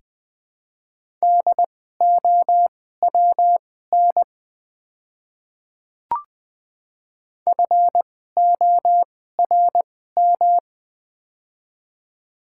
Morse code, heard as a keyed tone, about 21 wpm, spaced as for 15 wpm, 700 Hz. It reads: DOWN FORM
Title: 2x Character Spacing - Top 300 Words - 20wpm